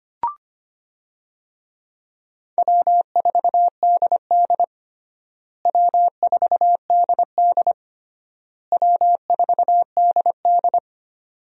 Morse code, heard as a keyed tone, about 25 wpm, 700 Hz. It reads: W4DD W4DD W4DD
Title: USA Call Signs - 4 Characters - Repeated 3x - 25wpm